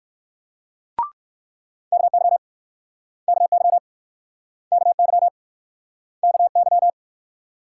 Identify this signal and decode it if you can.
Morse code, heard as a keyed tone, about 62 wpm, 690 Hz. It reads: KY KY KY KY